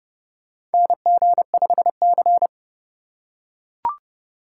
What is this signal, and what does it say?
Morse code, heard as a keyed tone, about 30 wpm, 700 Hz. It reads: NG5C
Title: USA Call Signs - 4x Character Spacing - 30wpm